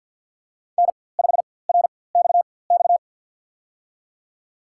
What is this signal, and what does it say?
Morse code, heard as a keyed tone, about 50 wpm, spaced as for 23 wpm, 700 Hz. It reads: N5RXX
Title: USA Call Signs - 5 Characters - 4x Character Spacing - 50wpm